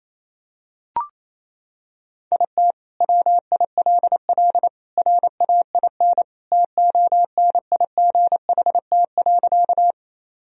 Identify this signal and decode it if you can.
Morse code, heard as a keyed tone, about 28 wpm, 700 Hz. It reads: IT WILL RAIN TONIGHT.
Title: Sentences from Top 1000 Words - Review Part 1 - 28wpm